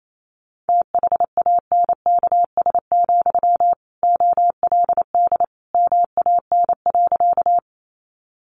Morse code, heard as a keyed tone, about 28 wpm, 700 Hz. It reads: THANKS, OLD MAN.
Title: Sentences from Top 600 Words 28wpm